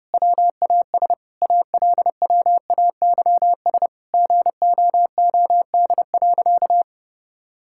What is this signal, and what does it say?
Morse code, heard as a keyed tone, about 32 wpm, 700 Hz. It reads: WAS ALWAYS GOOD.
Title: Sentences from Top 1000 Words - Review Part 5 - 30wpm